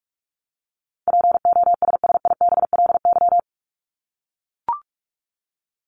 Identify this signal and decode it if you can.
Morse code, heard as a keyed tone, about 44 wpm, 700 Hz. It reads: POSSIBLY
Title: LY Suffix - Top 100 Words with Base Words 45wpm